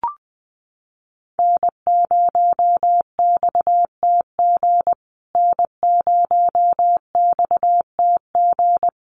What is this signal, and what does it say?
Morse code, heard as a keyed tone, about 20 wpm, 700 Hz. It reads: N0XTG N0XTG
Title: USA Call Signs Repeated 3x 20wpm